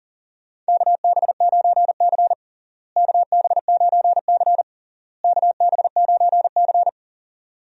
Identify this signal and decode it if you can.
Morse code, heard as a keyed tone, about 40 wpm, 700 Hz. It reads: KB9C KB9C KB9C
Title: USA Call Signs - 4 Characters - Repeated 3x - 40wpm